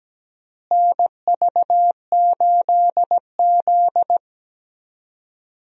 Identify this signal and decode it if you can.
Morse code, heard as a keyed tone, about 17 wpm, 700 Hz. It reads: NV8Z
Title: USA Call Signs - 4 Characters - Repeated 3x - 17wpm